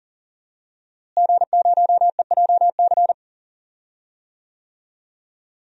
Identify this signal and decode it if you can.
Morse code, heard as a keyed tone, about 40 wpm, 700 Hz. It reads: G0EJC
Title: DXCC-100 Call Signs - 5 Characters - Repeated 3x - 40wpm